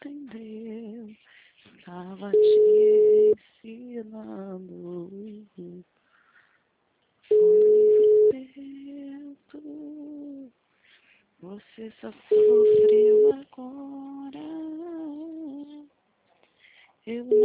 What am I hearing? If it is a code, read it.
Ringback tone (Europe, 425 Hz), with speech.